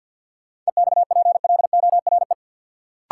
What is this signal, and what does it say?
Morse code, heard as a keyed tone, about 53 wpm, 700 Hz. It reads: EXPLORE
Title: Sentences from Top 1100 Words - New Word Drill Alt - 50wpm